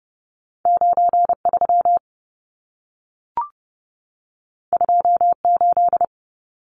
Morse code, heard as a keyed tone, about 30 wpm, 700 Hz. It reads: 93 28
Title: Numbers - 2-Digit 30wpm